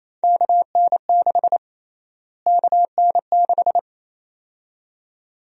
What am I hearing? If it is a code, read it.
Morse code, heard as a keyed tone, about 28 wpm, 700 Hz. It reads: KN6 KN6
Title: USA Call Sign Prefixes - Repeated 2x - 28wpm